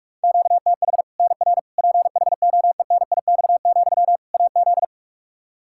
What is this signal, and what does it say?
Morse code, heard as a keyed tone, about 41 wpm, 700 Hz. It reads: QTH NR PHOENIX, AZ
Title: Practice QSO - Exchange 1 - Beginning - 45wpm